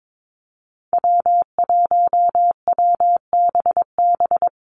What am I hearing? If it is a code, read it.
Morse code, heard as a keyed tone, about 22 wpm, 700 Hz. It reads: W1WBB
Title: Call Signs Repeated 2x - 22wpm